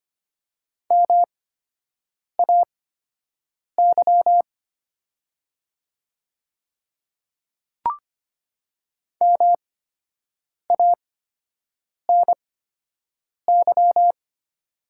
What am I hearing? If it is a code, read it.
Morse code, heard as a keyed tone, about 25 wpm, spaced as for 7 wpm, 700 Hz. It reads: MAY MANY
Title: 8x Character Spacing - Top 100 Words - 25wpm